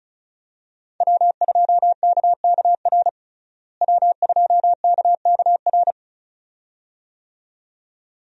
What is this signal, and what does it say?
Morse code, heard as a keyed tone, about 36 wpm, 700 Hz. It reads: W2KKR W2KKR
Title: USA Call Signs - 5 Characters - Repeated 2x - 35wpm